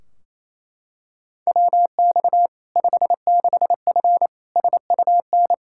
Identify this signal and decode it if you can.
Morse code, heard as a keyed tone, about 27 wpm, 700 Hz. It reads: WX 56F SUN
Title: Practice QSO - Exchange 2 - Info - 28wpm